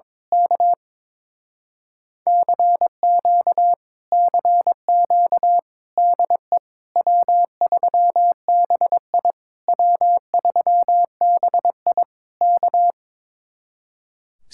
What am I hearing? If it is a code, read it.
Morse code, heard as a keyed tone, about 22 wpm, 700 Hz. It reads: K CQ CQ DE W3BI W3BI K
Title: Speed-Racing Common QSO Phrases Encore - 22wpm